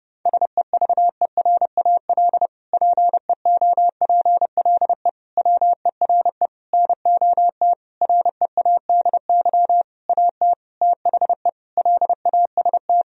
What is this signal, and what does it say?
Morse code, heard as a keyed tone, about 30 wpm, 700 Hz. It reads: SEVERAL PEOPLE WERE NOT READY AT THE LAST
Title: Sentences from Top 1000 Words - Review Part 2 - 30wpm